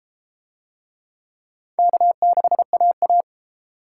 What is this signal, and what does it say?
Morse code, heard as a keyed tone, about 30 wpm, 700 Hz. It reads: K6AA